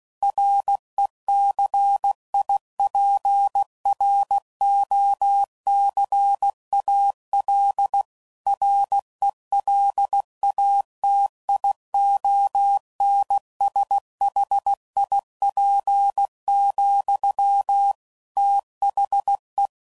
Morse code, heard as a keyed tone, about 16 wpm, 800 Hz. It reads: RECIPROCAL RELATIONSHIP, THE
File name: CW Code Groups @15WPM / Technical Subjects / XC